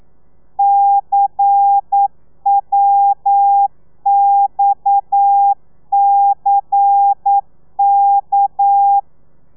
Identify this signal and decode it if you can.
Morse code, heard as a keyed tone, about 9 wpm, 800 Hz. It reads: CWXCK
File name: telegrafia / i2tfj / Lez11 40